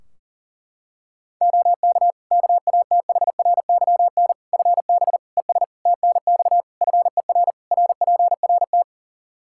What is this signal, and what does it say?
Morse code, heard as a keyed tone, about 40 wpm, 700 Hz. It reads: OK KATHRYN FB ES TNX FER RPRT
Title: K1USN SST QSO Practice 40wpm